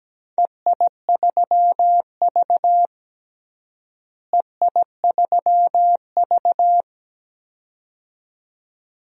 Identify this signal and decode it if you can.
Morse code, heard as a keyed tone, about 17 wpm, 700 Hz. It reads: EI3V EI3V
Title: DXCC-100 Call Signs - 4 Characters - Repeated 2x - 17wpm